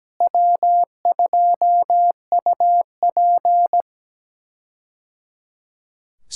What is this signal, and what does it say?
Morse code, heard as a keyed tone, about 17 wpm, 700 Hz. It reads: W2UP
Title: K1USN SST QSO Practice - 17wpm